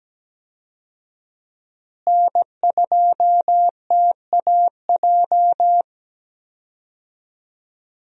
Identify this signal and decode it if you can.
Morse code, heard as a keyed tone, about 17 wpm, 700 Hz. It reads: N2TAJ
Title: USA Call Signs - 5 Characters - Repeated 2x - 17wpm